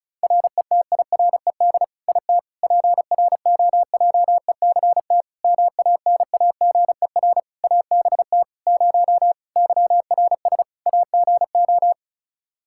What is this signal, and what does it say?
Morse code, heard as a keyed tone, about 36 wpm, 700 Hz. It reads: RETIRED IT PROJECT MANAGER ABT 0 YRS AGO
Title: Practice QSO - Exchange 3 - More Info - 35wpm